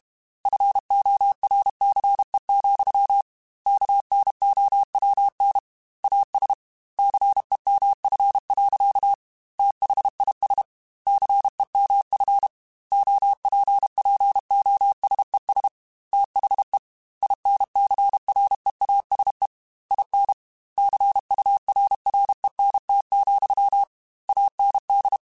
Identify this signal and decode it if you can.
Morse code, heard as a keyed tone, about 32 wpm, 800 Hz. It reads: FORCE, KNOWN AS CEMF. THIS CEMF OPPOSES THE INCREASE IN CURRENT, AND
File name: Technical Subjects / Technical Subjects-mp3s / XL @30